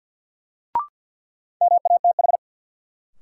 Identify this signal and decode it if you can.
Morse code, heard as a keyed tone, about 50 wpm, 700 Hz. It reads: MATH
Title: Sentences from Top 1000 Words - Review Part 6 - New Word Drill - 50wpm